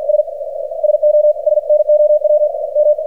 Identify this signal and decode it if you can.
Morse code, heard as a keyed tone, about 47 wpm, 600 Hz, with radio noise. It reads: S SO SNOW M
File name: archived / CRSnet